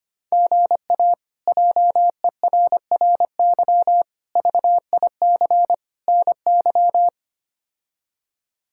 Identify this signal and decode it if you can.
Morse code, heard as a keyed tone, about 25 wpm, 700 Hz. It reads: GA JERRY VIC NY
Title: K1USN SST QSO Practice - 25wpm